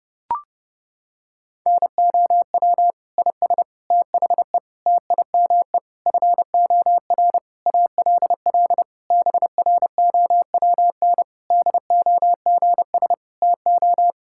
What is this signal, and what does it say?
Morse code, heard as a keyed tone, about 30 wpm, 700 Hz. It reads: NOW IS THE TIME FOR ALL BROWN DOGS TO
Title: Pangram Challenge v2 - 30wpm